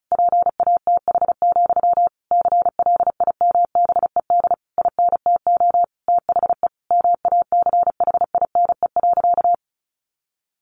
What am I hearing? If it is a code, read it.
Morse code, heard as a keyed tone, about 34 wpm, 700 Hz. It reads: PATH, CLIMBED INTO THE MACHINE.
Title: A Sound Of Thunder 35wpm